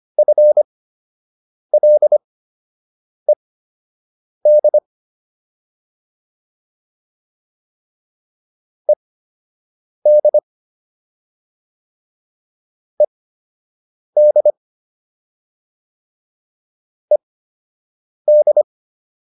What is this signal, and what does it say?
Morse code, heard as a keyed tone, about 25 wpm, spaced as for 7 wpm, 600 Hz. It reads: FLED ED ED ED